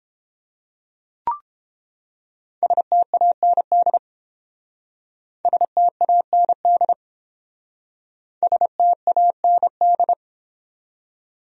Morse code, heard as a keyed tone, about 29 wpm, 700 Hz. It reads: STAND STAND STAND